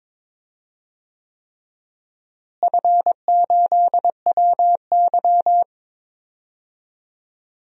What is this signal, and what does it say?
Morse code, heard as a keyed tone, about 22 wpm, 700 Hz. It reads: F8WY
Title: DXCC-100 Call Signs - 4 Characters - Repeated 3x - 22wpm